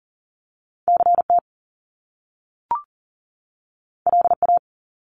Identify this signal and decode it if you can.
Morse code, heard as a keyed tone, about 39 wpm, 700 Hz. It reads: CT LA
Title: US States 40wpm